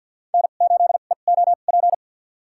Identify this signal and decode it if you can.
Morse code, heard as a keyed tone, about 47 wpm, spaced as for 35 wpm, 700 Hz. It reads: N8EOP
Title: USA Call Signs - 2x Character Spacing - 50wpm